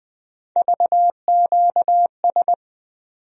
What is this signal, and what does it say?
Morse code, heard as a keyed tone, about 21 wpm, 700 Hz. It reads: VQS
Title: USA Call Sign Postfixes - 3 Letters 20wpm